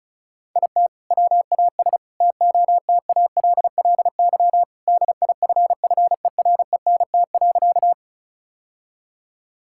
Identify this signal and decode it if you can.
Morse code, heard as a keyed tone, about 34 wpm, 700 Hz. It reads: IT WAS TOTALLY DIFFERENT.
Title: Sentences from Top 1600 Words 35wpm